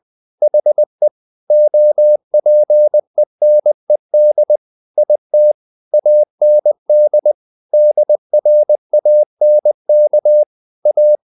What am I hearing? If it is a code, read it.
Morse code, heard as a keyed tone, about 20 wpm, 600 Hz. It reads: HE OPENED IT AND DRANK A
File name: the old man and the sea / the old man and the sea part 1 20wpm